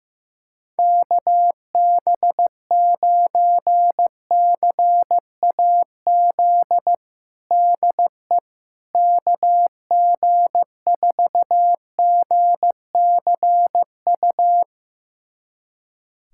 Morse code, heard as a keyed tone, about 16 wpm, 700 Hz. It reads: KB9CAZ DE KG4GCU
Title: USA Callsign DE Callsign 15wpm